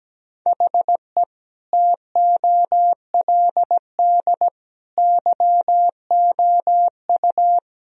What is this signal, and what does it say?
Morse code, heard as a keyed tone, about 17 wpm, 700 Hz. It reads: HE TOLD YOU
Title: Sentences from Top 1000 Words - Review Part 2 - 17wpm